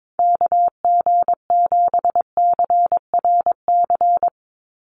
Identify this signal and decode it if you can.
Morse code, heard as a keyed tone, about 22 wpm, 700 Hz. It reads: KG7CRC
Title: USA Call Signs 22wpm